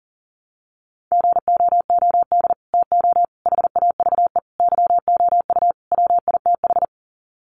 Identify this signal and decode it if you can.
Morse code, heard as a keyed tone, about 40 wpm, 700 Hz. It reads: GOOD TO HAVE YOU WITH
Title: Sets of 5 Words Encore2 40wpm